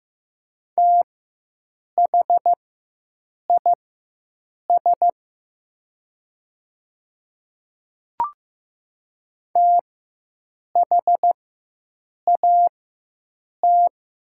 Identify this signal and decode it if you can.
Morse code, heard as a keyed tone, about 15 wpm, spaced as for 7 wpm, 700 Hz. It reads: THIS THAT
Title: 4x Character Spacing - Top 100 Words - 15wpm